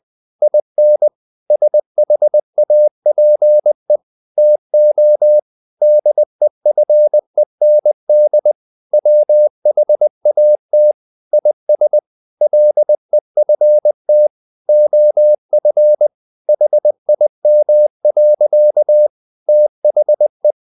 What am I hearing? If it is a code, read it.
Morse code, heard as a keyed tone, about 20 wpm, 600 Hz. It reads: IN SHAPE TO DEFEND WHAT IS LEFT OF HIM. THE
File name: the old man and the sea part 2 20wpm